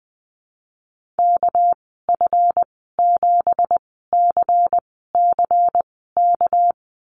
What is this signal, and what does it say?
Morse code, heard as a keyed tone, about 20 wpm, spaced as for 15 wpm, 700 Hz. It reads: KF7CCK
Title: 200 Lesson 32g - Letter k - Call Signs 20wpm